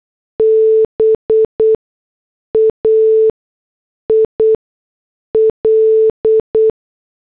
Morse code, heard as a keyed tone, about 8 wpm, spaced as for 6 wpm, 440 Hz. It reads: BAIL
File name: morse / headlines